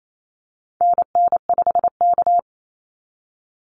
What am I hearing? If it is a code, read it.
Morse code, heard as a keyed tone, about 28 wpm, 700 Hz. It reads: NN5K
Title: NA SOTA Activator Callsigns 28wpm